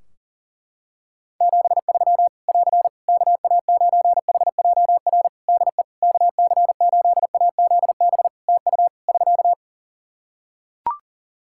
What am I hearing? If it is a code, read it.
Morse code, heard as a keyed tone, about 40 wpm, 700 Hz. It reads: 73 <AR> KA9HJR DE KC8AZB TU <SK>
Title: Practice QSO - Exchange 4 - Ending - 40wpm